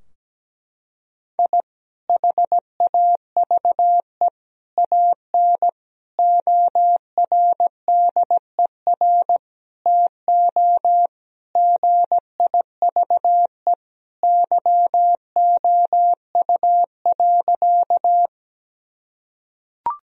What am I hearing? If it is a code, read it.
Morse code, heard as a keyed tone, about 17 wpm, 700 Hz. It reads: I HAVE AN ORDER TO GIVE YOU.
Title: Sentences from Top 1000 Words - Review Part 2 - 17wpm